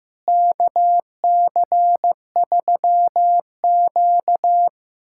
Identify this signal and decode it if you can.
Morse code, heard as a keyed tone, about 16 wpm, 700 Hz. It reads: KC3Q
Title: USA Callsign DE Callsign - 4 Characters - 15wpm